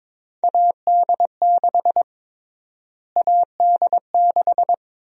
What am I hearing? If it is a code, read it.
Morse code, heard as a keyed tone, about 21 wpm, 700 Hz. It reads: AD6 AD6
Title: USA Call Sign Prefixes - Repeated 2x - 22wpm